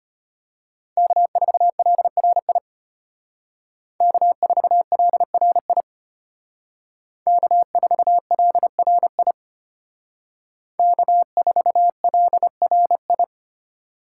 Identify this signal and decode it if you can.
Morse code, heard as a keyed tone, about 32 wpm, 700 Hz. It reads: K4LRI K4LRI K4LRI K4LRI